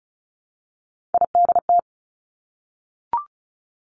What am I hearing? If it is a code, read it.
Morse code, heard as a keyed tone, about 36 wpm, 700 Hz. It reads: IDT
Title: Three Challenging Letter-Number Combinations 35wpm